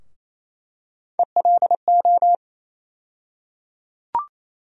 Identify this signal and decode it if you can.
Morse code, heard as a keyed tone, about 28 wpm, 700 Hz. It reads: ELO
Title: USA Call Sign Postfixes - 3 Letters 28wpm